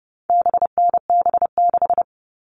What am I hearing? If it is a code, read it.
Morse code, heard as a keyed tone, about 30 wpm, 700 Hz. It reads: BNB6
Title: All Cut Numbers - 4-Digit 30wpm